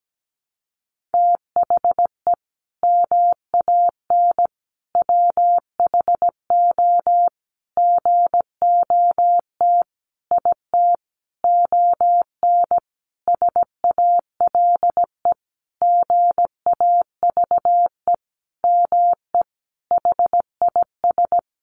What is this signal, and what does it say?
Morse code, heard as a keyed tone, about 17 wpm, 700 Hz. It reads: THE MAN WHO GOT IT ON SALE GAVE ME HIS
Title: Sentences from Top 1400 Words 17wpm